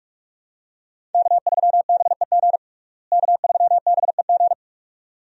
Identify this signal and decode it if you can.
Morse code, heard as a keyed tone, about 42 wpm, 700 Hz. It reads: K3BEG K3BEG
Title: USA Call Signs - 5 Characters - Repeated 2x - 45wpm